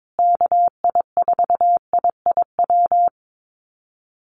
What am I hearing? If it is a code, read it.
Morse code, heard as a keyed tone, about 22 wpm, 700 Hz. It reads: KI4IIW
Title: USA Call Signs Repeated 3x 22wpm